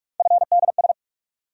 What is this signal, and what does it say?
Morse code, heard as a keyed tone, about 45 wpm, 700 Hz. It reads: FDS
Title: USA Call Sign Postfixes - 3 Letters 45wpm